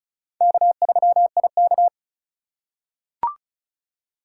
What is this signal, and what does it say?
Morse code, heard as a keyed tone, about 34 wpm, 700 Hz. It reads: K3IK
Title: USA Call Signs - 4 Characters - 4x Character Spacing - 35wpm